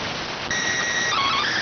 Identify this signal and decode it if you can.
SSTV (slow-scan television), PD 290.